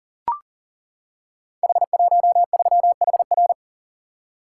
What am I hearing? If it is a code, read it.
Morse code, heard as a keyed tone, about 39 wpm, 690 Hz. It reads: H13HR
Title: Code Groups - 5 Letter:Numbers - 40wpm